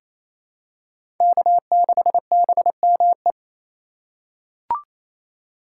Morse code, heard as a keyed tone, about 27 wpm, 700 Hz. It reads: K6BME
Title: USA Call Signs - 5 Characters - 4x Character Spacing - 28wpm